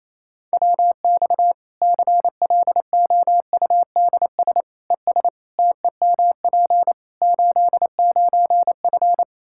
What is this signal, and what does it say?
Morse code, heard as a keyed tone, about 28 wpm, 700 Hz. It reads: WX CLOUDS ES TEMP 89F
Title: Practice QSO - Exchange 2 - Info - 28wpm